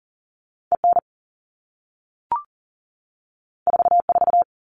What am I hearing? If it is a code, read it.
Morse code, heard as a keyed tone, about 40 wpm, 700 Hz. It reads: EN 44
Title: All Cut Numbers - 2-Digit 40wpm